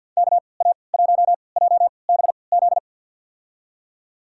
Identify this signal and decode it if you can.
Morse code, heard as a keyed tone, about 52 wpm, spaced as for 29 wpm, 700 Hz. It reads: KA1JBZ
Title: USA Call Signs - 6 Characters - 3x Character Spacing - 50wpm